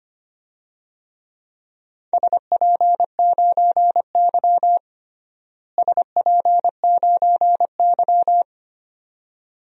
Morse code, heard as a keyed tone, about 25 wpm, 700 Hz. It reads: SP9Y SP9Y
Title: DXCC-100 Call Signs - 4 Characters - Repeated 2x - 25wpm